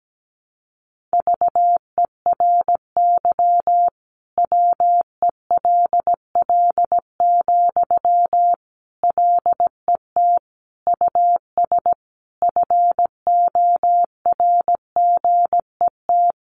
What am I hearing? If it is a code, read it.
Morse code, heard as a keyed tone, about 17 wpm, 700 Hz. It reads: VERY WELL, LET US FORGET
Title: Sentences from Top 1200 Words 17wpm